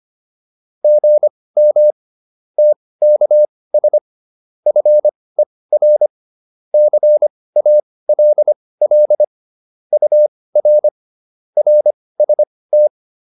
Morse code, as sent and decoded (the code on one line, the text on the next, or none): --. -- / - -.- ... / ..-. .-. / -.-. .- .-.. .-.. / ..- .-. / .-. ... -
GM TKS FER CALL UR RST